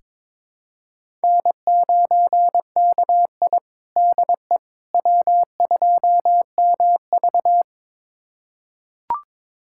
Morse code, heard as a keyed tone, about 21 wpm, 700 Hz. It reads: N9KI DE W2MV
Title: USA Callsign DE Callsign - 4 Characters - 22wpm